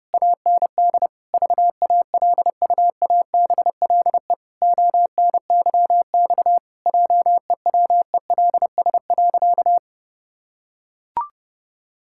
Morse code, heard as a keyed tone, about 30 wpm, 700 Hz. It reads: AND VALUABLE ONYX JEWELS.
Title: Pangram Challenge v2 - 30wpm